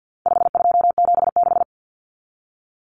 Morse code, heard as a keyed tone, about 50 wpm, 700 Hz. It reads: 5276
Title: Numbers - 4-Digit 50wpm